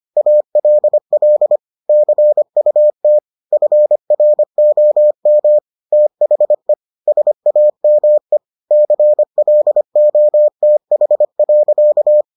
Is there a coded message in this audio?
Yes: Morse code from 0.2 to 12.2 s, heard as a keyed tone, about 25 wpm, 600 Hz. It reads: ALL CUT FROM THE SAME CLOTH.